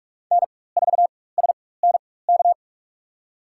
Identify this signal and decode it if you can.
Morse code, heard as a keyed tone, about 46 wpm, spaced as for 21 wpm, 700 Hz. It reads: N4SNK
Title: USA Call Signs - 5 Characters - 4x Character Spacing - 45wpm